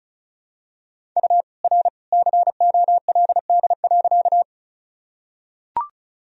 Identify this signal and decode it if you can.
Morse code, heard as a keyed tone, about 36 wpm, 700 Hz. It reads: U R COLD.
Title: Sentences from Top 400 Words and Abbreviations 35wpm